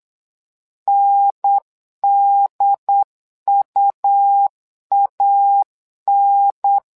Morse code, heard as a keyed tone, about 8 wpm, 800 Hz. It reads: NDUAN
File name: LEZIONI / 2 UHDN / 2 UHDN 8wpm2232